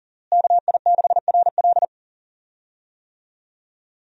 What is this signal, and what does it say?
Morse code, heard as a keyed tone, about 37 wpm, 700 Hz. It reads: KI6RL
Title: USA Call Signs - 5 Characters - Repeated 2x - 40wpm